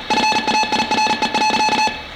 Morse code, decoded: FAIRE.